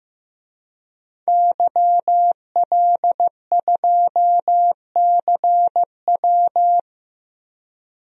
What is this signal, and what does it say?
Morse code, heard as a keyed tone, about 15 wpm, 700 Hz. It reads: YL2CW
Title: DXCC-100 Call Signs - 5 Characters - 15wpm